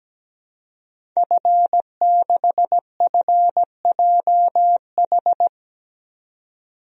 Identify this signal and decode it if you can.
Morse code, heard as a keyed tone, about 17 wpm, 700 Hz. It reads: F6FJH
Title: DXCC-100 Call Signs - 5 Characters - 17wpm